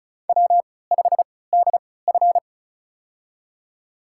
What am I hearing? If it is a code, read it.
Morse code, heard as a keyed tone, about 36 wpm, spaced as for 20 wpm, 700 Hz. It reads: W5DF
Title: USA Call Signs - 4 Characters - 3x Character Spacing - 35wpm